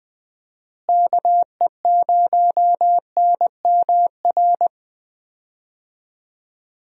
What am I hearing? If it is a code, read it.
Morse code, heard as a keyed tone, about 20 wpm, 700 Hz. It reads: KE0NMR
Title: USA Call Signs - 6 Characters - Repeated 2x - 20wpm